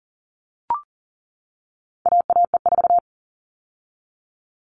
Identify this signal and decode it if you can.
Morse code, heard as a keyed tone, about 40 wpm, 700 Hz. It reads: AAE4
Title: All Cut Numbers - 4-Digit 40wpm